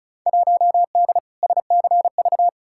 Morse code, heard as a keyed tone, about 36 wpm, 700 Hz. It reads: 1D SCV
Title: Field Day Exchanges Encore - 35wpm